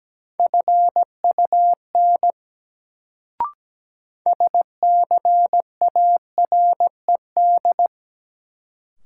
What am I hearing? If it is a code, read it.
Morse code, heard as a keyed tone, about 17 wpm, 700 Hz. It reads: FUN SCARED
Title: Sentences from Top 700 Words - New Word Drill - 17wpm